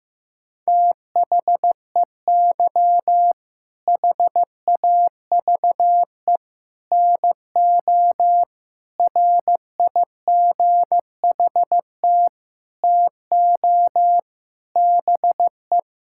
Morse code, heard as a keyed tone, about 15 wpm, 700 Hz. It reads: THEY HAVE NO RIGHT TO BE
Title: Sentences from Top 1000 Words - Review Part 6 - 15wpm